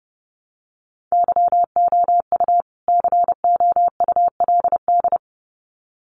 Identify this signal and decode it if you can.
Morse code, heard as a keyed tone, about 28 wpm, 700 Hz. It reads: YOU COULD
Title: Sets of 2 Words 30wpm